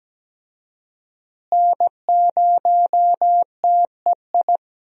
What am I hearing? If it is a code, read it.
Morse code, heard as a keyed tone, about 17 wpm, 700 Hz. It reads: N0TEI
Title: USA Call Signs - 5 Characters - Repeated 3x - 17wpm